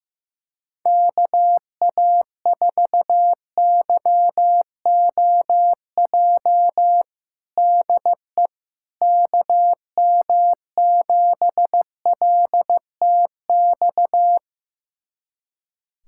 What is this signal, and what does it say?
Morse code, heard as a keyed tone, about 15 wpm, 700 Hz. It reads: KA4YOJ DE KM7LTX
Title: USA Callsign DE Callsign 15wpm